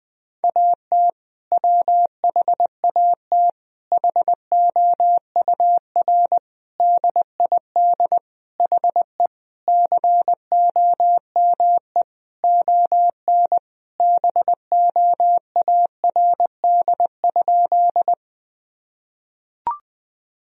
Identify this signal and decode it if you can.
Morse code, heard as a keyed tone, about 20 wpm, 700 Hz. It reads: AT WHAT HOUR DID HE COME ON BOARD?
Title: Sentences from Top 1000 Words - Review Part 4 - 20wpm